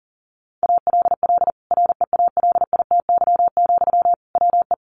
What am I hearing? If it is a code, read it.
Morse code, heard as a keyed tone, about 40 wpm, 700 Hz. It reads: ALL REALITY, WE
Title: Sentences from Top 1400 Words 40wpm